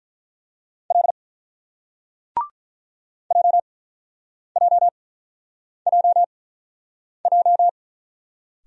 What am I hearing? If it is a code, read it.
Morse code, heard as a keyed tone, about 47 wpm, spaced as for 9 wpm, 690 Hz. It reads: L JJJJ